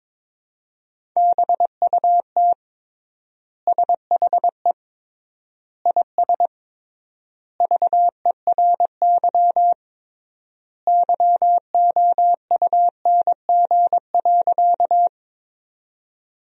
Morse code, heard as a keyed tone, about 22 wpm, 700 Hz. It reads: BUT SHE IS VERY YOUNG.
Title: 3x Word-Spacing - Sentences from Top 400 Words - 22wpm